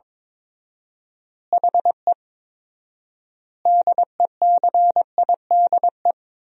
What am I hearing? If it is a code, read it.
Morse code, heard as a keyed tone, about 22 wpm, 700 Hz. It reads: HE DECIDE
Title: 4x Word-Spacing - Sentences from Top 500 Words - 22wpm